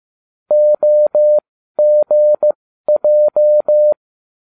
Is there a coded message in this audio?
Yes: Morse code from 0.5 to 3.9 s, heard as a keyed tone, about 15 wpm, spaced as for 12 wpm, 600 Hz. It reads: OGJ